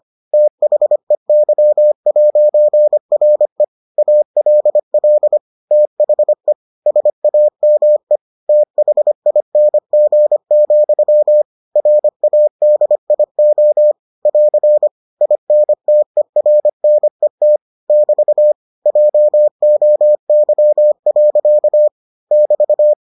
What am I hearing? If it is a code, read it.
Morse code, heard as a keyed tone, about 25 wpm, 600 Hz. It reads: THEY'RE ALL THE SAME THING, RADIO <AR> INTERNET <BT> JOY. <BT>